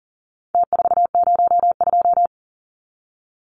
Morse code, heard as a keyed tone, about 40 wpm, 700 Hz. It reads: T402
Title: Numbers - Contest Style - 4-Digit 40wpm